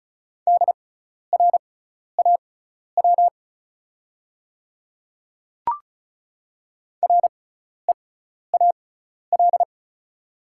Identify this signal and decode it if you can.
Morse code, heard as a keyed tone, about 36 wpm, spaced as for 12 wpm, 690 Hz. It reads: DRAW REAL
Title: 6x Character Spacing - Top 300 Words - 35wpm